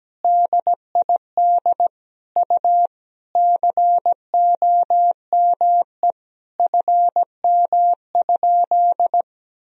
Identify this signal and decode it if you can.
Morse code, heard as a keyed tone, about 17 wpm, 700 Hz. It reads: DID U COME FM?
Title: Sentences from Top 200 Words and Abbreviations 17wpm